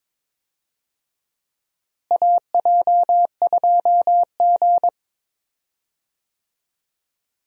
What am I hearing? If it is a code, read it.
Morse code, heard as a keyed tone, about 22 wpm, 700 Hz. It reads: AJ2G
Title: USA Call Signs - 4 Characters - Repeated 2x - 22wpm